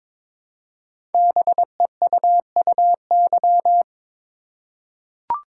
Morse code, heard as a keyed tone, about 22 wpm, 700 Hz. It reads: BEUUY
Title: Code Groups - 5 Letters - 22wpm